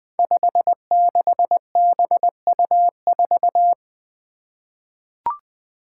Morse code, heard as a keyed tone, about 21 wpm, 700 Hz. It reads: 56BU4
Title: Five Challenging Letter-Number Combinations 20wpm